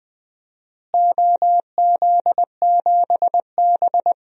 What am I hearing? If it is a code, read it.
Morse code, heard as a keyed tone, about 21 wpm, 700 Hz. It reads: OZ7B
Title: DXCC-100 Call Signs - 4 Characters - Repeated 3x - 20wpm